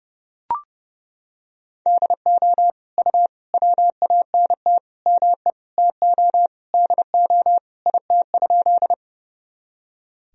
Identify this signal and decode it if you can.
Morse code, heard as a keyed tone, about 30 wpm, 700 Hz. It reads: DO U WANT ME TO DO IT?
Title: Sentences from Top 200 Words and Abbreviations 30wpm